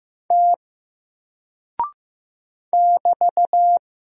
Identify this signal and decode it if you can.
Morse code, heard as a keyed tone, about 15 wpm, 700 Hz. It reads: T <BT>
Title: Single Character (Letter:Number:Punctuation:Prosign) 15wpm